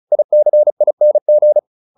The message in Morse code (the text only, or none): ICING